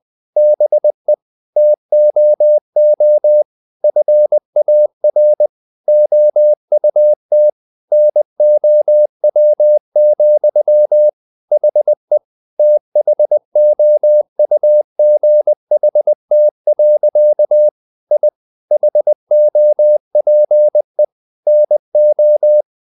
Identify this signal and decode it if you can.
Morse code, heard as a keyed tone, about 20 wpm, 600 Hz. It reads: BE TOO FAR OUT NOW, HE THOUGHT. I HOPE NO
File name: the old man and the sea part 2 20wpm